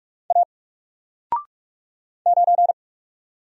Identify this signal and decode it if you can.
Morse code, heard as a keyed tone, about 44 wpm, 700 Hz. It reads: A 9
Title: Single Letter-Number Alt 45wpm